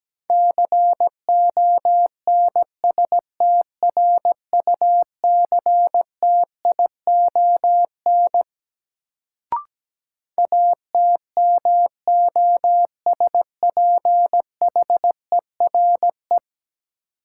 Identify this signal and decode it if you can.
Morse code, heard as a keyed tone, about 17 wpm, 700 Hz. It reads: CONSTRUCTION ATMOSPHERE
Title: Sentences from Top 1600 Words - New Word Drill Alt - 17wpm